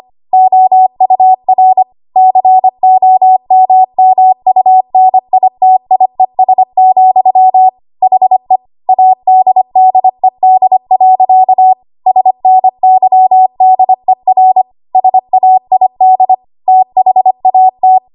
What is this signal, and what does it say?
Morse code, heard as a keyed tone, about 25 wpm, 750 Hz. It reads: OUR COMMUNITIES, HE ADDED. SNYDER SAID THAT
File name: Morse / 25WPM